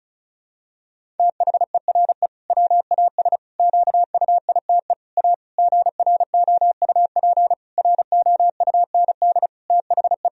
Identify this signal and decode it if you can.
Morse code, heard as a keyed tone, about 36 wpm, 700 Hz. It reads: THERE WAS QUITE A GROUP ROUND THE
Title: Sentences from Top 1000 Words - Review Part 1 - 35wpm